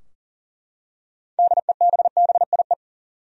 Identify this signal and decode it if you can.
Morse code, heard as a keyed tone, about 40 wpm, 700 Hz. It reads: DEBBIE
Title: USA Common Names - 40wpm